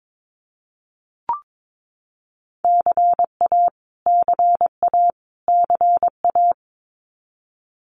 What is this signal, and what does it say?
Morse code, heard as a keyed tone, about 22 wpm, 700 Hz. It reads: CA CA CA
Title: US States Repeated 3x 22wpm